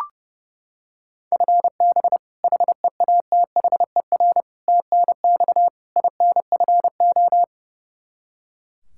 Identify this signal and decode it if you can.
Morse code, heard as a keyed tone, about 30 wpm, 700 Hz. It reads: FB HEATHER TNX INFO
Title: Practice QSO - Exchange 3 - More Info - 30wpm